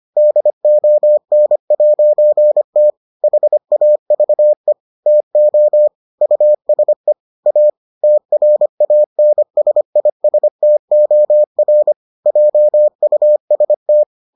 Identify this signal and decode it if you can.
Morse code, heard as a keyed tone, about 25 wpm, 600 Hz. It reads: DON'T HAVE TO USE A TRANSISTOR JUST